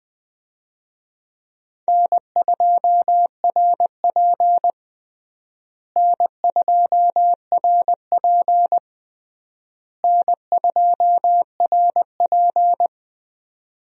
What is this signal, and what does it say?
Morse code, heard as a keyed tone, about 21 wpm, 700 Hz. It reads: N2RP N2RP N2RP
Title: USA Call Signs - 4 Characters - Repeated 3x - 20wpm